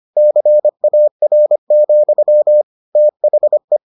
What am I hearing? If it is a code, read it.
Morse code, heard as a keyed tone, about 25 wpm, 600 Hz. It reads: CAR, THE